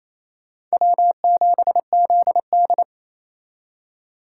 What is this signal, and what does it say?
Morse code, heard as a keyed tone, about 36 wpm, 700 Hz. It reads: W7ZD